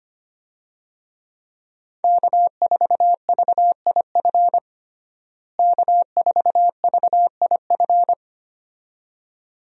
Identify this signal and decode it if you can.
Morse code, heard as a keyed tone, about 25 wpm, 700 Hz. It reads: K4VIF K4VIF
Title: USA Call Signs - 5 Characters - Repeated 2x - 25wpm